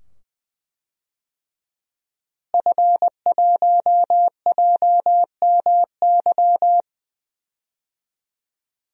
Morse code, heard as a keyed tone, about 20 wpm, 700 Hz. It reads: F1JMY
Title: DXCC-100 Call Signs - 5 Characters - Repeated 2x - 20wpm